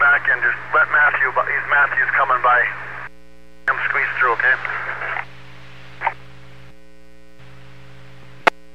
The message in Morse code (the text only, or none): none